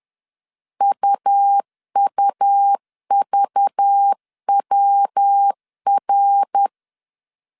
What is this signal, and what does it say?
Morse code, heard as a keyed tone, about 10 wpm, 800 Hz. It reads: UUVWR
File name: CW / DET / 1 9 1500C S B 10wpm 4439